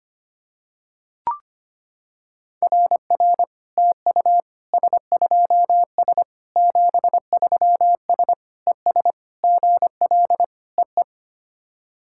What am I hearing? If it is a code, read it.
Morse code, heard as a keyed tone, about 25 wpm, 690 Hz. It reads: RR TU S2S 73S ES GL EE
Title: SOTA QSO Practice 25wpm